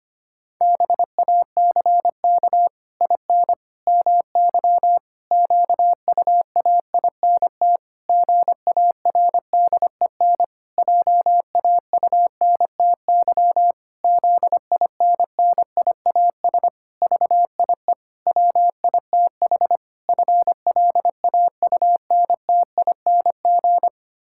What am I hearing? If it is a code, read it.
Morse code, heard as a keyed tone, about 25 wpm, 700 Hz. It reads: BACK IN MY QUAINT GARDEN JAUNTY ZINNIAS VIE WITH FLAUNTING
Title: Pangram Challenge v2 - 25wpm